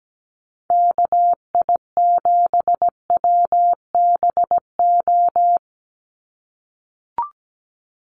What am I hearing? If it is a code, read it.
Morse code, heard as a keyed tone, about 17 wpm, 700 Hz. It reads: KI7WBO
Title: USA Call Signs 17wpm